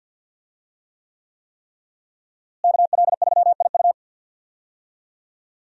Morse code, heard as a keyed tone, about 46 wpm, 700 Hz. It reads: KL3IU
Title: USA Call Signs - 5 Characters - Repeated 3x - 50wpm